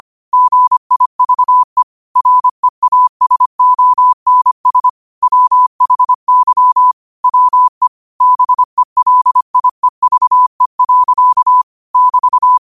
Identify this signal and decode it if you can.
Morse code, heard as a keyed tone, about 25 wpm, 1000 Hz. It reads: GIVE REASONS WHY WE BELIEVE. <BT>